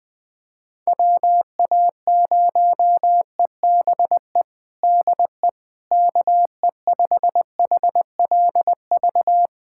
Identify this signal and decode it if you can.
Morse code, heard as a keyed tone, about 20 wpm, 700 Hz. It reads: WA0EBE DE KE5HLV
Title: USA Callsign DE Callsign - 6 Characters - 20wpm